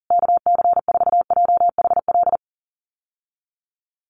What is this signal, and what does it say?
Morse code, heard as a keyed tone, about 40 wpm, 700 Hz. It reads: KC4JHL
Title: USA Call Signs Repeated 2x 40wpm